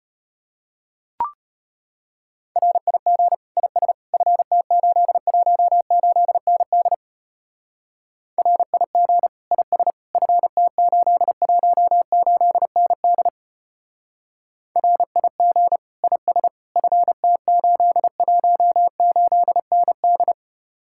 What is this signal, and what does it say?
Morse code, heard as a keyed tone, about 31 wpm, 700 Hz. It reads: RIG IS FT818ND RIG IS FT818ND RIG IS FT818ND
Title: Speed-Racing Common QSO Phrases - 25wpm